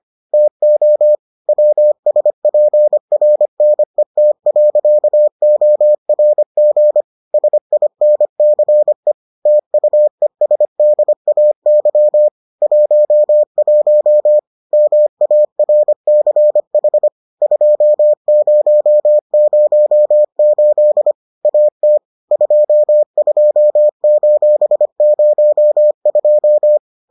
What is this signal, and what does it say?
Morse code, heard as a keyed tone, about 25 wpm, 600 Hz. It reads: TO WSPRNET.ORG SINCE TUESDAY 11 MARCH 2008 AT 22:02